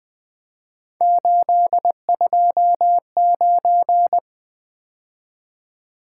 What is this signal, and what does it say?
Morse code, heard as a keyed tone, about 20 wpm, 700 Hz. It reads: 829
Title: Three Letter-Number Combinations - 20wpm